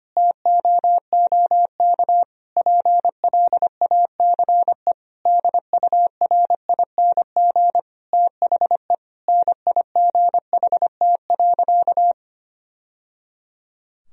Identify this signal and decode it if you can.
Morse code, heard as a keyed tone, about 25 wpm, 700 Hz. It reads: TOOK PLACE DURING THE NIGHT.
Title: Sentences from Top 1000 Words - Review Part 4 - 25wpm